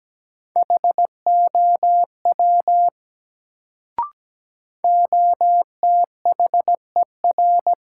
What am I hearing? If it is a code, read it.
Morse code, heard as a keyed tone, about 17 wpm, 700 Hz. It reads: HOW OTHER
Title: Sentences from Top 100 Words - New Word Drill - 17wpm